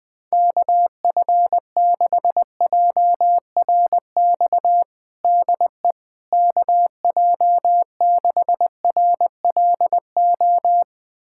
Morse code, heard as a keyed tone, about 20 wpm, 700 Hz. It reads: KF6JRX DE KJ6RLO
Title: USA Callsign DE Callsign 20wpm